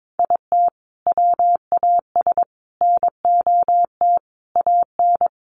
Morse code, heard as a keyed tone, about 22 wpm, 700 Hz. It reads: IT WAS NOT AN
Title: Sentences from Top 1300 Words 22wpm